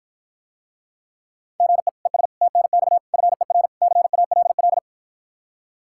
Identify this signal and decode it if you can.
Morse code, heard as a keyed tone, about 54 wpm, 700 Hz. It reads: GE ES TNX FER CALL